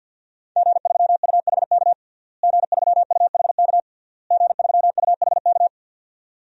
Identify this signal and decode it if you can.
Morse code, heard as a keyed tone, about 52 wpm, 700 Hz. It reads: G3UHK G3UHK G3UHK
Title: DXCC-100 Call Signs - 5 Characters - Repeated 3x - 50wpm